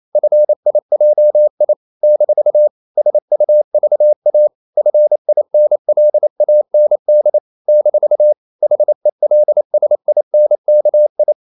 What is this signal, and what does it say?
Morse code, heard as a keyed tone, about 28 wpm, 600 Hz. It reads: FIJI - SUVA FINLAND - HELSINKI
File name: CW / Practice / NationalCapitals28